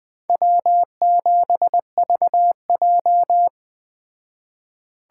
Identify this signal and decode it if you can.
Morse code, heard as a keyed tone, about 21 wpm, 700 Hz. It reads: W7VJ